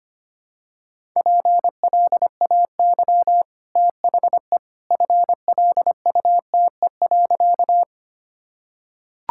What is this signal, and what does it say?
Morse code, heard as a keyed tone, about 25 wpm, 700 Hz. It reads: PLAY THE FLUTE.
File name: Easy Sentences with Hints - 25wpm